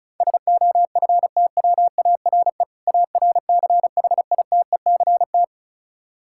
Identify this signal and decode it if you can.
Morse code, heard as a keyed tone, about 36 wpm, 700 Hz. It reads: SOFTWARE ARCHITECT
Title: Practice QSO - Exchange 3 - More Info - 35wpm